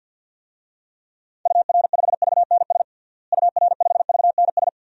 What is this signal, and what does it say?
Morse code, heard as a keyed tone, about 47 wpm, 700 Hz. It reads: UR5VNS UR5VNS
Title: DXCC-100 Call Signs - 6 Characters - Repeated 2x - 50wpm